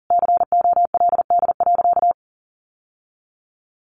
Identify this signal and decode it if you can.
Morse code, heard as a keyed tone, about 40 wpm, 700 Hz. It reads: COLD.
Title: Sentences from Top 1000 Words 40wpm